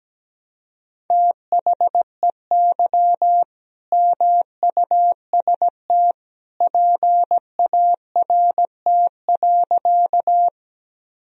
Morse code, heard as a keyed tone, about 17 wpm, 700 Hz. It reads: THEY MUST PART.
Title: Sentences from Top 1000 Words - Review Part 1 - 17wpm